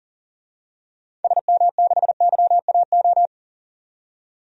Morse code, heard as a keyed tone, about 40 wpm, 700 Hz. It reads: SM6YAO
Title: DXCC-100 Call Signs - 6 Characters - 40wpm